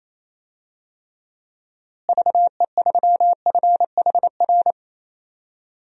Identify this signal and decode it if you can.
Morse code, heard as a keyed tone, about 28 wpm, 700 Hz. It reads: VE3FHR
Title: DXCC-100 Call Signs - 6 Characters - Repeated 2x - 28wpm